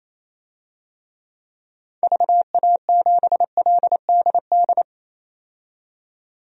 Morse code, heard as a keyed tone, about 28 wpm, 700 Hz. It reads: VA7LDD
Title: DXCC-100 Call Signs - 6 Characters - 28wpm